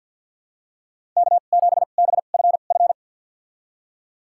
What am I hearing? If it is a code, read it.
Morse code, heard as a keyed tone, about 51 wpm, spaced as for 36 wpm, 700 Hz. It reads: K7BFF
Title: USA Call Signs - 5 Characters - 2x Character Spacing - 50wpm